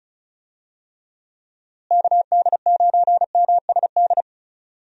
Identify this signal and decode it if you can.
Morse code, heard as a keyed tone, about 36 wpm, 700 Hz. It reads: KD9MSD
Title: USA Call Signs - 6 Characters - Repeated 2x - 35wpm